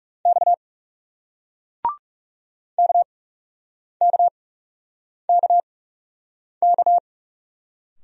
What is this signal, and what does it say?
Morse code, heard as a keyed tone, about 41 wpm, spaced as for 8 wpm, 700 Hz. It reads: X KKKK